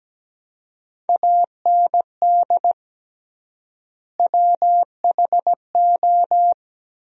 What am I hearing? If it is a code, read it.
Morse code, heard as a keyed tone, about 17 wpm, 700 Hz. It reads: AND WHO
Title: 3x Word-Spacing - Sentences from Top 100 Words - 17wpm